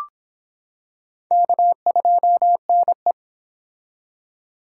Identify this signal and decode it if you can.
Morse code, heard as a keyed tone, about 26 wpm, 700 Hz. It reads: K2NE